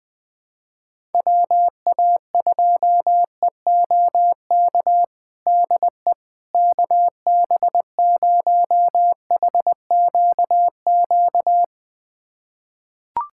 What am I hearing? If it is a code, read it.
Morse code, heard as a keyed tone, about 20 wpm, 700 Hz. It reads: WA2EOK DE KB0HQQ
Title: USA Callsign DE Callsign - 6 Characters - 20wpm